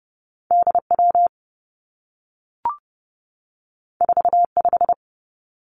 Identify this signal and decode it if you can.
Morse code, heard as a keyed tone, about 30 wpm, 700 Hz. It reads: D W 4 5
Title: Two Challenging Letter-Number Combinations 30wpm